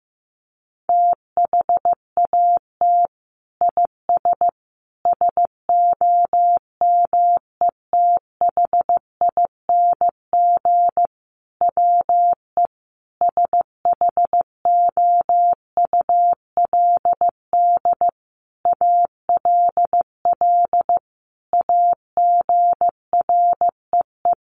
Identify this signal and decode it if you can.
Morse code, heard as a keyed tone, about 15 wpm, 700 Hz. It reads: THAT IS SOMETHING WE SHOULD ALL AGREE
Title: Sentences from Top 1100 Words 15wpm